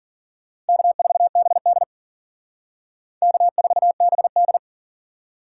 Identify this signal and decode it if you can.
Morse code, heard as a keyed tone, about 53 wpm, 700 Hz. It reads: K4BD K4BD